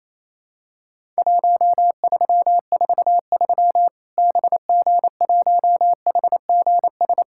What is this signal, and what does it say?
Morse code, heard as a keyed tone, about 28 wpm, 700 Hz. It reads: 1343 BG1HGS
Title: Worked All Europe DX Contest - QTC Exchange - 28wpm